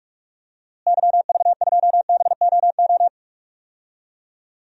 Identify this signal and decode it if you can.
Morse code, heard as a keyed tone, about 42 wpm, 700 Hz. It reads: YV2BOO
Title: DXCC-100 Call Signs - 6 Characters - Repeated 2x - 45wpm